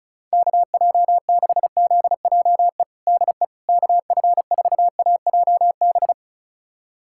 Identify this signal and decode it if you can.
Morse code, heard as a keyed tone, about 36 wpm, 700 Hz. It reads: KJ6ZJE DE KF4AJB
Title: USA Callsign DE Callsign 35wpm